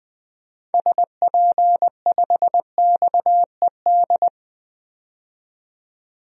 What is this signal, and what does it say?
Morse code, heard as a keyed tone, about 20 wpm, 700 Hz. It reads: SP5XED
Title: DXCC-100 Call Signs - 6 Characters - 20wpm